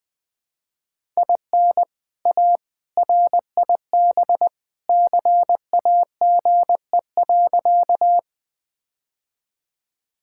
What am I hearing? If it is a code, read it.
Morse code, heard as a keyed tone, about 20 wpm, 700 Hz. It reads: IN A RIB CAGE.
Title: All-Age Appropriate Jokes - 20wpm